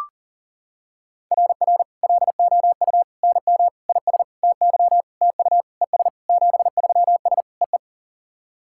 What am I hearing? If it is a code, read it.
Morse code, heard as a keyed tone, about 40 wpm, 700 Hz. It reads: RR LOU NM IS TY TU ES 73S EE
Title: SOTA QSO Practice 40wpm